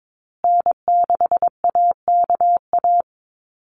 Morse code, heard as a keyed tone, about 22 wpm, 700 Hz. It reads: N6AKA
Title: USA Call Signs Repeated 2x 22wpm